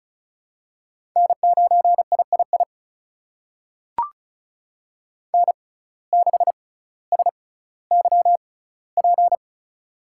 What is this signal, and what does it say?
Morse code, heard as a keyed tone, about 36 wpm, spaced as for 18 wpm, 700 Hz. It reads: N9III N6SYP
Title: USA Call Signs - 5 Characters - 6x Character Spacing - 35wpm